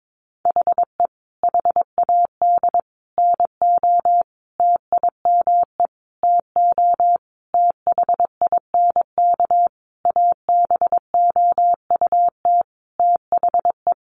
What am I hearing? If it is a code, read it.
Morse code, heard as a keyed tone, about 22 wpm, 700 Hz. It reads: HE HAD NO TIME TO THINK ABOUT THE
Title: Sentences from Top 700 Words 22wpm